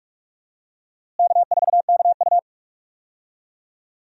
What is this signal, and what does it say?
Morse code, heard as a keyed tone, about 42 wpm, 700 Hz. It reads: K4KU
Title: USA Call Signs - 4 Characters - Repeated 3x - 45wpm